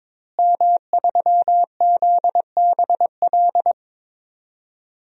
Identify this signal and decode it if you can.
Morse code, heard as a keyed tone, about 22 wpm, 700 Hz. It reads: M3ZBL
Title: DXCC-100 Call Signs - 5 Characters - Repeated 2x - 22wpm